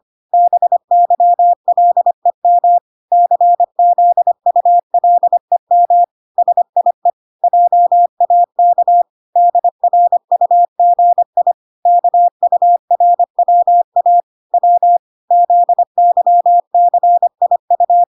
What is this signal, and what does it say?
Morse code, heard as a keyed tone, about 25 wpm, 700 Hz. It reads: BYLEM CZULEM SIE JAK DRUGI KURWA W ZYCIU